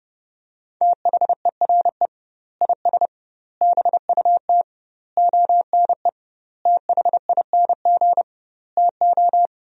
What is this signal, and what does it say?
Morse code, heard as a keyed tone, about 32 wpm, 700 Hz. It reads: THERE IS BUT ONE THING TO
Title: 2x Word-Spacing - Sentences from Top 100 Words - 30wpm